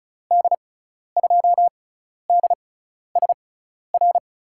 Morse code, heard as a keyed tone, about 34 wpm, spaced as for 12 wpm, 700 Hz. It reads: D2DSR
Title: USA Call Signs - 6 Characters - 6x Character Spacing - 35wpm